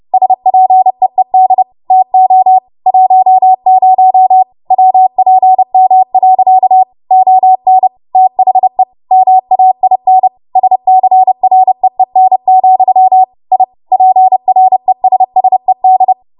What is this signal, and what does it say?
Morse code, heard as a keyed tone, about 30 wpm, 750 Hz. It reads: SPEED TO 10 WPM. ON THE MAIN SCREEN, I PRESSED
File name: Morse / 30WPM